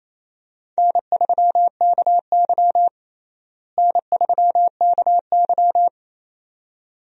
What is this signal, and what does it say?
Morse code, heard as a keyed tone, about 28 wpm, 700 Hz. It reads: N3KY N3KY
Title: USA Call Signs - 4 Characters - Repeated 2x - 28wpm